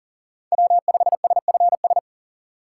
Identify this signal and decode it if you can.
Morse code, heard as a keyed tone, about 48 wpm, 700 Hz. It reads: W5SFS
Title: Speed Racing - USA 1x3 Call Signs 40wpm